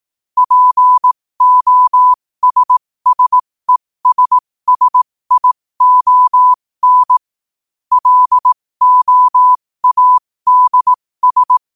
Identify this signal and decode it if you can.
Morse code, heard as a keyed tone, about 18 wpm, spaced as for 15 wpm, 1000 Hz. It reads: POSSESSION LOADS
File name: qotdfiles / 15wpm